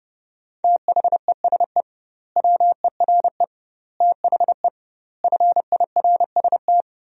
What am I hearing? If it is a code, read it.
Morse code, heard as a keyed tone, about 30 wpm, 700 Hz. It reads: THESE WERE THE FIRST
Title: 2x Word-Spacing - Sentences from Top 300 Words - 30wpm